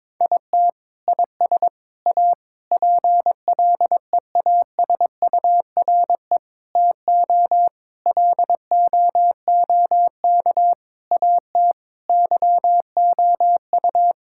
Morse code, heard as a keyed tone, about 22 wpm, 700 Hz. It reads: IT IS A PLEASURE TO LOOK AT YOU
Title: Sentences from Top 1600 Words 22wpm